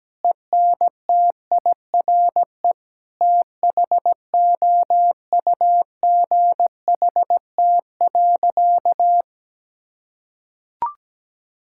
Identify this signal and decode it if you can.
Morse code, heard as a keyed tone, about 17 wpm, 700 Hz. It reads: ENTIRE THOUGHT.
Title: Sentences from Top 1000 Words - Review Part 5 - 17wpm